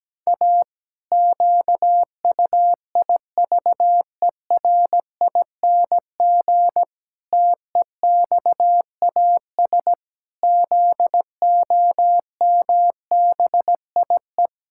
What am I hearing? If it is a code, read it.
Morse code, heard as a keyed tone, about 17 wpm, 700 Hz. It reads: A QUIVERING TEXAS ZOMBIE
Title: Pangram Challenge v2 - 17wpm